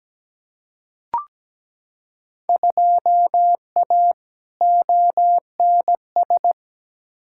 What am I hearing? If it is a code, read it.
Morse code, heard as a keyed tone, about 17 wpm, 700 Hz. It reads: 2A ONS
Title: Field Day Exchanges Encore - 17wpm